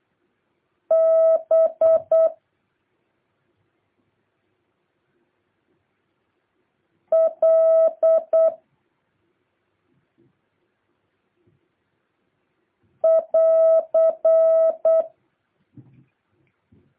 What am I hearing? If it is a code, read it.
Morse code, heard as a keyed tone, about 8 wpm, 1300 Hz. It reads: B L <AR>
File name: jh8cbh / 2023cw